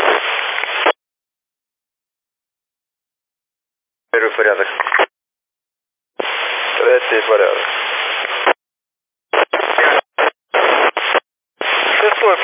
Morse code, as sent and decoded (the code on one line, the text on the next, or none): none